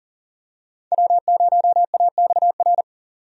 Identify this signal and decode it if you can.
Morse code, heard as a keyed tone, about 40 wpm, 700 Hz. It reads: W0AXR
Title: USA Call Signs - 5 Characters - 40wpm